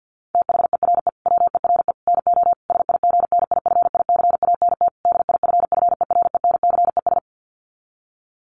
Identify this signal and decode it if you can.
Morse code, heard as a keyed tone, about 48 wpm, 700 Hz. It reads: THERE WERE NO SIGNIFICANT DIFFERENCES
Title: Sets of 5 Words Encore2 50wpm